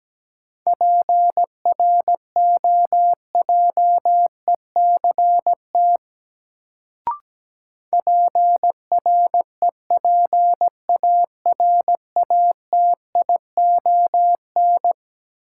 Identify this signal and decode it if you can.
Morse code, heard as a keyed tone, about 17 wpm, 700 Hz. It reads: PROJECT PREPARATION
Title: Sentences from Top 1400 Words - New Word Drill - 17wpm